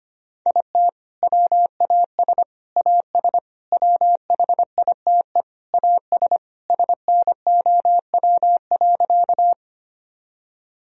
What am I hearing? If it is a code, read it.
Morse code, heard as a keyed tone, about 25 wpm, 700 Hz. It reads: IT WAS AS WHITE AS SNOW.
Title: Sentences from Top 1000 Words - Review Part 3 - 25wpm